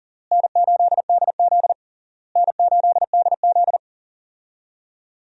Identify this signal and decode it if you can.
Morse code, heard as a keyed tone, about 40 wpm, 700 Hz. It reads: N8DZ N8DZ
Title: USA Call Signs - 4 Characters - Repeated 3x - 40wpm